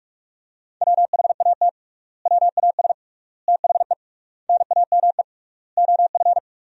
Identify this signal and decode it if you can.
Morse code, heard as a keyed tone, about 47 wpm, 700 Hz. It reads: WHAT WAS THE NAME OF
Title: 3x Word-Spacing - Sentences from Top 300 Words - 45wpm